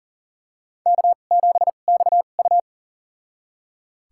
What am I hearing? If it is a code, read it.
Morse code, heard as a keyed tone, about 40 wpm, spaced as for 29 wpm, 700 Hz. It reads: K7XU
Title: USA Call Signs - 2x Character Spacing - 40wpm